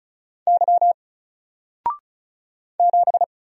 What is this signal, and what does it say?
Morse code, heard as a keyed tone, about 36 wpm, 700 Hz. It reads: Y 7
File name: Single-Letter-Number-35wpm